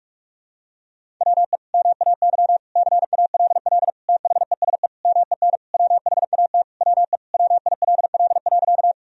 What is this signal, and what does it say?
Morse code, heard as a keyed tone, about 43 wpm, 700 Hz. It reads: WE MAY CALL THESE MEN WHAT WE WILL.
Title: Sentences from Top 1000 Words - Review Part 1 - 45wpm